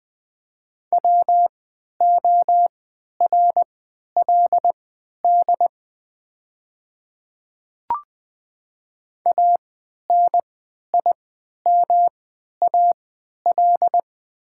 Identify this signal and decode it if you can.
Morse code, heard as a keyed tone, about 20 wpm, spaced as for 11 wpm, 700 Hz. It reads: WORLD ANIMAL
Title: 3x Character Spacing - Top 200 Words - 20wpm